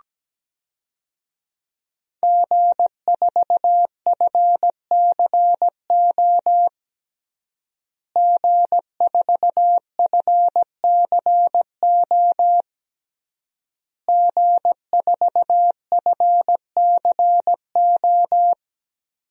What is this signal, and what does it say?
Morse code, heard as a keyed tone, about 17 wpm, 700 Hz. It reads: G4FCO G4FCO G4FCO
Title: DXCC-100 Call Signs - 5 Characters - Repeated 3x - 17wpm